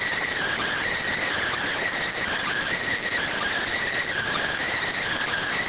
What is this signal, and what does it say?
Emergency Alert System (SAME) tones (header bursts), heard as harsh buzzing bursts. It is heard from 0.2 to 4.7 s.